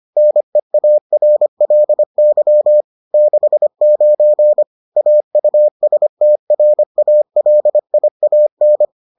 Morse code, heard as a keyed tone, about 25 wpm, 600 Hz. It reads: NEARLY 69 AUSTRALIAN